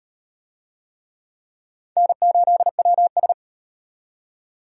Morse code, heard as a keyed tone, about 38 wpm, 700 Hz. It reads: N8WS